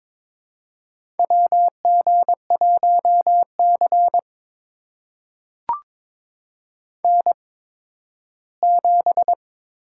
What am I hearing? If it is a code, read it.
Morse code, heard as a keyed tone, about 22 wpm, spaced as for 13 wpm, 700 Hz. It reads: WG1C N7
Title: USA Call Signs - 8x Character Spacing - 22wpm